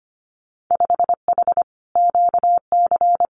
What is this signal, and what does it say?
Morse code, heard as a keyed tone, about 25 wpm, 700 Hz. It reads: 5H QC
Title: Winter Field Day Exchanges Encore 25wpm